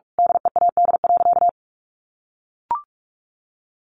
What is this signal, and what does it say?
Morse code, heard as a keyed tone, about 44 wpm, 700 Hz. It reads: DEAD.
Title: A Sound Of Thunder 45wpm